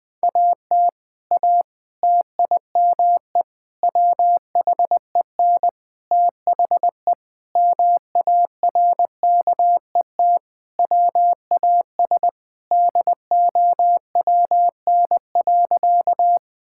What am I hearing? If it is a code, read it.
Morse code, heard as a keyed tone, about 20 wpm, 700 Hz. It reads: AT A TIME WHEN THE MARKET WAS DOWN.
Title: Sentences with 5-Word Phrases - 20wpm